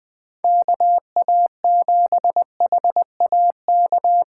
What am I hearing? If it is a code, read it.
Morse code, heard as a keyed tone, about 20 wpm, 700 Hz. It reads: KA7HAK
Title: USA Call Signs - 6 Characters - 2x Character Spacing - 20wpm